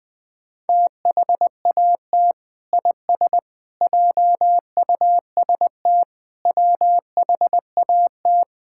Morse code, heard as a keyed tone, about 20 wpm, 700 Hz. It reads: THAT IS JUST WHAT
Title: Sentences from Top 1600 Words 20wpm